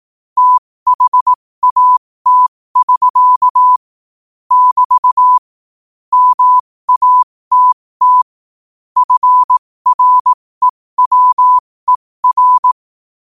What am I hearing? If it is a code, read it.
Morse code, heard as a keyed tone, about 18 wpm, spaced as for 15 wpm, 1000 Hz. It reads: THAT<SK> <BT> MATT FREWER
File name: qotdfiles / 15wpm